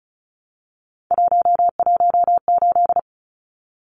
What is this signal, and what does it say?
Morse code, heard as a keyed tone, about 34 wpm, 700 Hz. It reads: 118
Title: Numbers - 3-Digit 35wpm